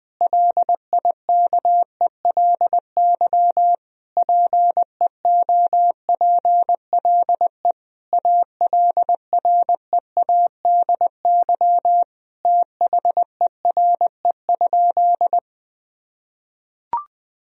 Morse code, heard as a keyed tone, about 20 wpm, 700 Hz. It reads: LIKELY PEOPLE ALREADY THERE?
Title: Sentences from Top 1000 Words - Review Part 4 - 20wpm